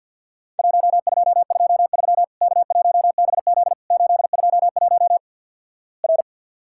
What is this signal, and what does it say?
Morse code, heard as a keyed tone, about 50 wpm, 700 Hz. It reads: 1223 K1BZ 821 R
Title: Worked All Europe DX Contest - QTC Exchange with Repeats - 50wpm